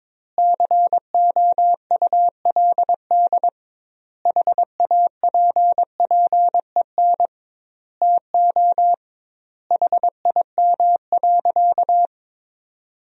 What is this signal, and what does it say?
Morse code, heard as a keyed tone, about 22 wpm, 700 Hz. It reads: COULD HAPPEN TO HIM.
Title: 2x Word-Spacing - Sentences from Top 400 Words - 22wpm